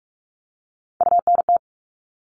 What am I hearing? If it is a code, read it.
Morse code, heard as a keyed tone, about 44 wpm, 700 Hz. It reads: UNT
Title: All Cut Numbers - 3-Digit 45wpm